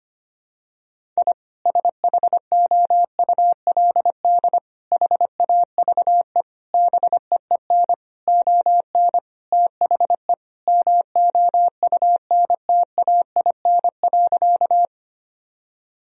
Morse code, heard as a keyed tone, about 25 wpm, 700 Hz. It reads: I SHOULD HAVE BEEN ON THE MOUNTAIN.
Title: Sentences from Top 1000 Words - Review Part 1 - 25wpm